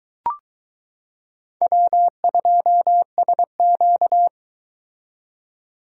Morse code, heard as a keyed tone, about 22 wpm, 700 Hz. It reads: W2SQ